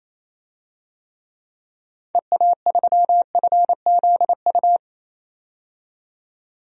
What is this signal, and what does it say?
Morse code, heard as a keyed tone, about 28 wpm, 700 Hz. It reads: EA3FZU
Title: DXCC-100 Call Signs - 6 Characters - 28wpm